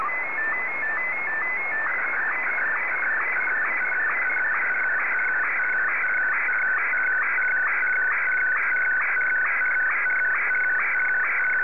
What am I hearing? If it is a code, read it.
SSTV (slow-scan television), Martin 1.
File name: sstv